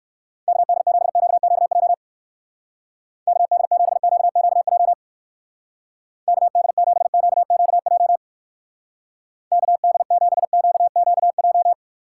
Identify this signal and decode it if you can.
Morse code, heard as a keyed tone, about 59 wpm, 700 Hz. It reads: KD7QQJ KD7QQJ KD7QQJ KD7QQJ